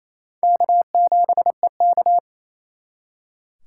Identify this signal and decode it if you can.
Morse code, heard as a keyed tone, about 34 wpm, 700 Hz. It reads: K7EK
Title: Speed Racing - USA 1x2 Call Signs 28wpm